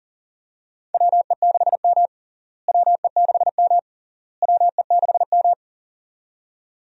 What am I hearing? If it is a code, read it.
Morse code, heard as a keyed tone, about 40 wpm, 700 Hz. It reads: WE6M WE6M WE6M
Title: USA Call Signs - 4 Characters - Repeated 3x - 40wpm